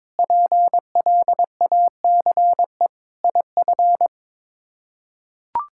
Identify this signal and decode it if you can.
Morse code, heard as a keyed tone, about 22 wpm, 700 Hz. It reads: PLACE IF
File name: practice / 22wpm / Sets of 8 Words - 22wpm